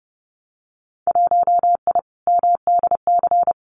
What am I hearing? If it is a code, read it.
Morse code, heard as a keyed tone, about 30 wpm, 700 Hz. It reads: 1I MDC
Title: Winter Field Day Exchanges 30wpm